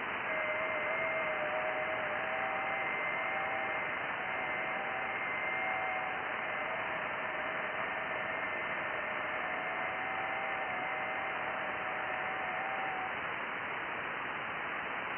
FT8.
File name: IC-7100